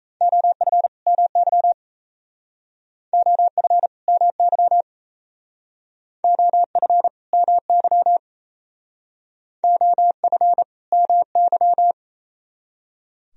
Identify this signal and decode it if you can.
Morse code, heard as a keyed tone, about 34 wpm, 700 Hz. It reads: OF MY OF MY OF MY OF MY
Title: Speed-Racing - Sets of 2 Words 28wpm